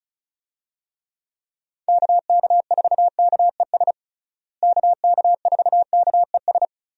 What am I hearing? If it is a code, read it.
Morse code, heard as a keyed tone, about 34 wpm, 700 Hz. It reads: KK4KES KK4KES
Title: USA Call Signs - 6 Characters - Repeated 2x - 35wpm